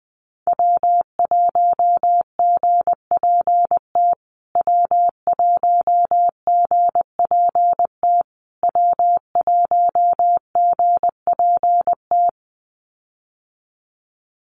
Morse code, heard as a keyed tone, about 20 wpm, 700 Hz. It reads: W1GPT W1GPT W1GPT
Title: USA Call Signs Repeated 3x 20wpm